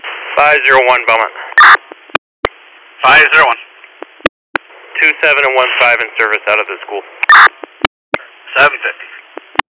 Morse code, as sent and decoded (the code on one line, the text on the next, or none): none